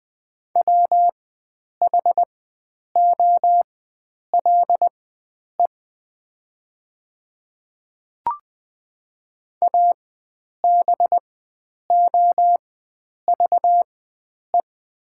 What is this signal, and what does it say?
Morse code, heard as a keyed tone, about 20 wpm, spaced as for 9 wpm, 700 Hz. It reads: WHOLE ABOVE
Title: 4x Character Spacing - Top 500 Words - 20wpm